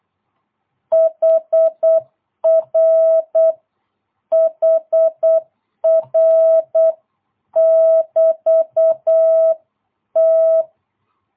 Morse code, heard as a keyed tone, about 8 wpm, 650 Hz. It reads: HR HR <BT> T
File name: jh8cbh / 2022cw